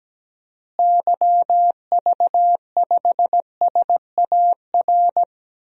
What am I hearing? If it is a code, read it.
Morse code, heard as a keyed tone, about 17 wpm, 700 Hz. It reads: YV5SAR
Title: DXCC-100 Call Signs - 6 Characters - Repeated 3x - 17wpm